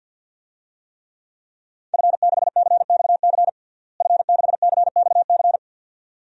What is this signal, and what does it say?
Morse code, heard as a keyed tone, about 52 wpm, 700 Hz. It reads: F6CXC F6CXC
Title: DXCC-100 Call Signs - 5 Characters - Repeated 2x - 50wpm